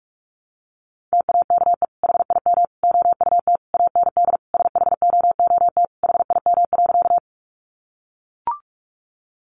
Morse code, heard as a keyed tone, about 44 wpm, 700 Hz. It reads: TAKE HIM OUT AND SHOOT HIM.
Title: Sentences from Top 1200 Words 45wpm